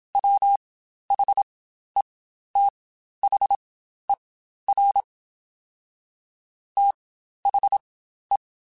Morse code, heard as a keyed tone, about 27 wpm, spaced as for 12 wpm, 800 Hz. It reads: WHETHER THE
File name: Spaced / Chuckles Spaced-mp3s / HalfFull sp @25